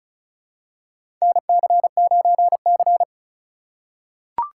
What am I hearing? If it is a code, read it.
Morse code, heard as a keyed tone, about 36 wpm, 700 Hz. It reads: NC9C
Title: USA Call Signs - 4x Character Spacing - 35wpm